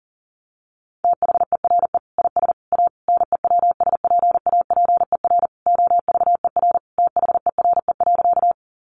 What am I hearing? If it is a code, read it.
Morse code, heard as a keyed tone, about 40 wpm, 700 Hz. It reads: THERE IS A NEWSPAPER OVER THERE.
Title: Sentences from Top 1100 Words 40wpm